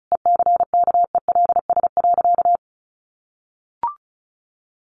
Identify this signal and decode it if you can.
Morse code, heard as a keyed tone, about 34 wpm, 700 Hz. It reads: ECKELS.
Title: A Sound Of Thunder 35wpm